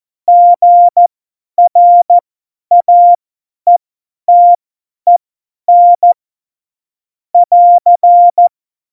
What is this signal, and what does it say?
Morse code, heard as a keyed tone, about 14 wpm, spaced as for 10 wpm, 700 Hz. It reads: GRAETEN <AR>
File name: Morsekurs / Witze / Witz 29. Mai 2018, 50 F70